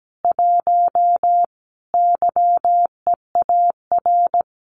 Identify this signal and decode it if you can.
Morse code, heard as a keyed tone, about 17 wpm, 700 Hz. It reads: 1 YEAR
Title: Common QSO Phrases Encore 17wpm